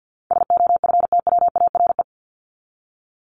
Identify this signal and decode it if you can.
Morse code, heard as a keyed tone, about 50 wpm, 700 Hz. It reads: SOFTWARE
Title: Top 501-750 Words 50wpm